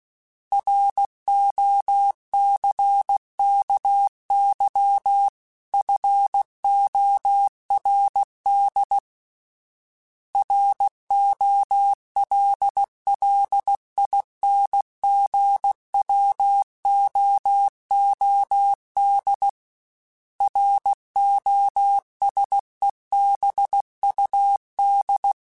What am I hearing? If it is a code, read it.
Morse code, heard as a keyed tone, about 16 wpm, 800 Hz. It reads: ROCKY FORD ROLLINGWOOD ROSEBUD
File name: fromSMs / FromSMs mp3s / Cities@15